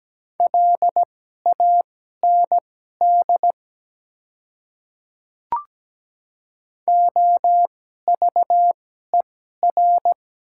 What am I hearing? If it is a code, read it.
Morse code, heard as a keyed tone, about 17 wpm, spaced as for 12 wpm, 690 Hz. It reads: LAND OVER
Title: 2x Character Spacing - Top 300 Words - 17wpm